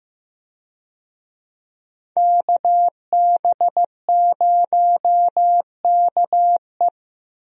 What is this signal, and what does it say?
Morse code, heard as a keyed tone, about 15 wpm, 700 Hz. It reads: KB0KE
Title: USA Call Signs - 5 Characters - 15wpm